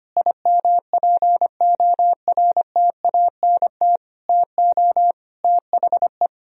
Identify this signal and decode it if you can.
Morse code, heard as a keyed tone, about 25 wpm, 700 Hz. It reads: IMPORTANT TO THE
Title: Sentences from Top 1000 Words - Review Part 4 - 25wpm